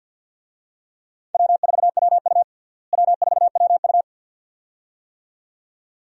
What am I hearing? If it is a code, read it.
Morse code, heard as a keyed tone, about 50 wpm, 700 Hz. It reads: W4WU W4WU
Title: USA Call Signs - 4 Characters - Repeated 2x - 50wpm